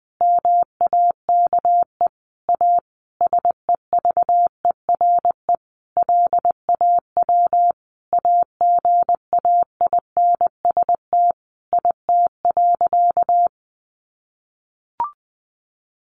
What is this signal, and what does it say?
Morse code, heard as a keyed tone, about 20 wpm, 700 Hz. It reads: MAKE A SEVERE LAW AGAINST IT.
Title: Sentences from Top 1400 Words 20wpm